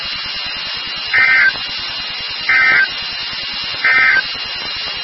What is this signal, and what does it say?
Emergency Alert System (SAME) tones, heard as harsh buzzing bursts. It is heard from 1.0 to 4.3 s.
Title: NOAA Weather Radio WXK36